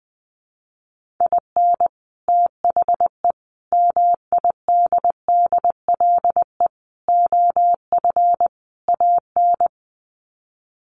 Morse code, heard as a keyed tone, about 20 wpm, 700 Hz. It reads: IN THE MIDDLE OF AN
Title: Sets of 5 Words Encore2 20wpm